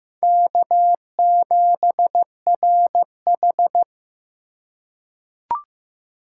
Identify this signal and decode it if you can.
Morse code, heard as a keyed tone, about 15 wpm, 700 Hz. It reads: K7RH
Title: USA Call Signs - 4 Characters - 4x Character Spacing - 15wpm